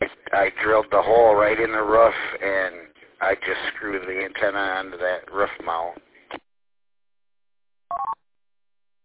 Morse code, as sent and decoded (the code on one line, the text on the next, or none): none